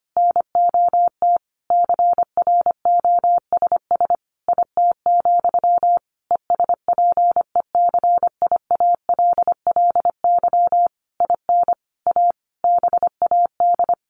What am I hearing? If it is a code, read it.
Morse code, heard as a keyed tone, about 25 wpm, 700 Hz. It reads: NOT CROSS IT, ESPECIALLY IN A BAD
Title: Sentences from Top 1400 Words 25wpm